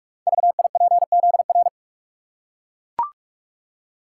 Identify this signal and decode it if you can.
Morse code, heard as a keyed tone, about 44 wpm, 700 Hz. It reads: VIPZR
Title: Code Groups - 5 Letters - 45wpm